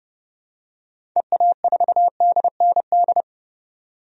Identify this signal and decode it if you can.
Morse code, heard as a keyed tone, about 30 wpm, 690 Hz. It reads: EA4DND
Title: DXCC-100 Call Signs - 6 Characters - 30wpm